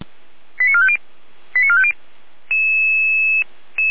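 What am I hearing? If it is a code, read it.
Five-tone selective call (ZVEI), heard as quick runs of five tones, tones of about 70 ms. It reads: 89450 89450
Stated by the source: ZVEI five-tone selective call received on an SDR.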